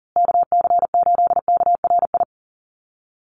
Morse code, heard as a keyed tone, about 40 wpm, 700 Hz. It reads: KC8KRI
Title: USA Call Signs Repeated 3x 40wpm